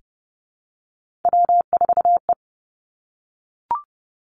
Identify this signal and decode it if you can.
Morse code, heard as a keyed tone, about 31 wpm, 700 Hz. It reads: W4E
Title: Three Challenging Letter-Number Combinations 30wpm